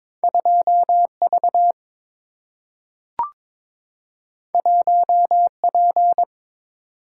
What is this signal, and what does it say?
Morse code, heard as a keyed tone, about 22 wpm, 700 Hz. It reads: 2 V 1 P
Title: Single-Letter Single-Number 22wpm